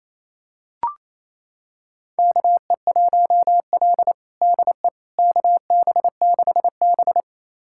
Morse code, heard as a keyed tone, about 28 wpm, 700 Hz. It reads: KE1L DE KB6B
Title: USA Callsign DE Callsign - 4 Characters - 28wpm